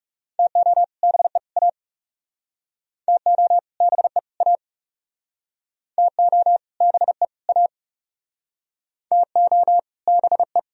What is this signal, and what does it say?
Morse code, heard as a keyed tone, about 39 wpm, 700 Hz. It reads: TO BE A TO BE A TO BE A TO BE